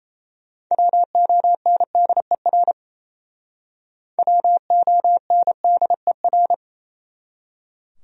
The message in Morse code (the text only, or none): WONDER WONDER